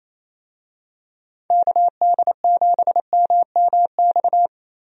Morse code, heard as a keyed tone, about 27 wpm, 700 Hz. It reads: KD7MMX
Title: USA Call Signs - 6 Characters - 28wpm